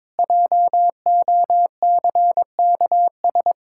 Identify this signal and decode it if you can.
Morse code, heard as a keyed tone, about 22 wpm, 700 Hz. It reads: JOCKS
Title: Pangram Challenge v2 - 22wpm